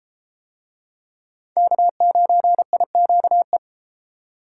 Morse code, heard as a keyed tone, about 30 wpm, 700 Hz. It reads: K9IQE